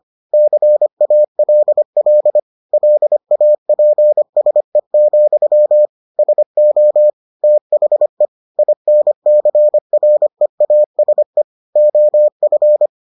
Morse code, heard as a keyed tone, about 25 wpm, 600 Hz. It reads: CALL LAPSE, SO THE INCREASE OF